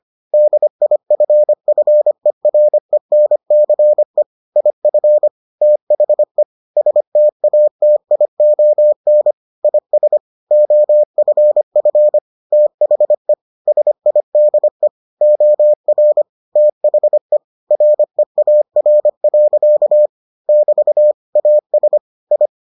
Morse code, heard as a keyed tone, about 25 wpm, 600 Hz. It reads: DIFFERENCE IF THE STATION IS OFF THE SIDE OR THE REAR. <BT> AS I